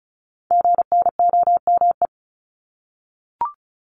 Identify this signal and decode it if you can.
Morse code, heard as a keyed tone, about 36 wpm, 700 Hz. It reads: GNOME
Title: Multi-letter Phonemes Repeated 3x 35wpm